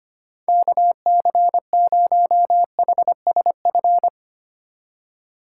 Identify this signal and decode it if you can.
Morse code, heard as a keyed tone, about 32 wpm, 700 Hz. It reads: KC0HSF